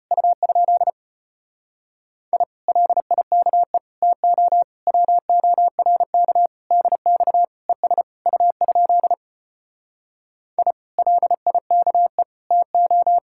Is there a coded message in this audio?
Yes: Morse code from 0.1 to 13.2 s, heard as a keyed tone, about 33 wpm, 700 Hz. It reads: U? I LIKE TO WORK DX ES U? I LIKE TO